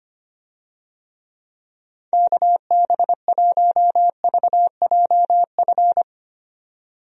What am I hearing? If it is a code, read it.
Morse code, heard as a keyed tone, about 25 wpm, 700 Hz. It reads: KB1VJF
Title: USA Call Signs - 6 Characters - 25wpm